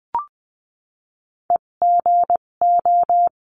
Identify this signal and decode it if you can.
Morse code, heard as a keyed tone, about 20 wpm, spaced as for 17 wpm, 700 Hz. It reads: EGO
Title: 257 Lesson 42b - Words 14-20wpm